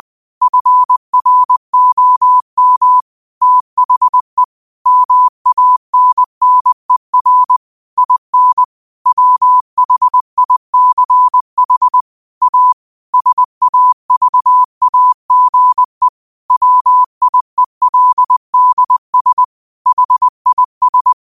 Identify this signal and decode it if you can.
Morse code, heard as a keyed tone, about 20 wpm, 1000 Hz. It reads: FROM THE MANNER IN WHICH A SAVAGE WIELDS HIS